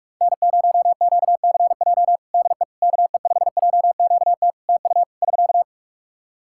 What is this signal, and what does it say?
Morse code, heard as a keyed tone, about 42 wpm, 700 Hz. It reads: N0QCJ DE KE5JQT TU <SK>
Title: Practice QSO - Exchange 4 - Ending - 45wpm